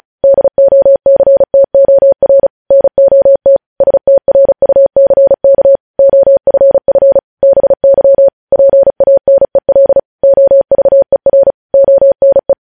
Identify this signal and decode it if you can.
Morse code, heard as a keyed tone, about 36 wpm, 550 Hz. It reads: DOCTOR NOT STRUCK OFF BY PANEL OVER ONE
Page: News headlines in Morse code at 35 wpm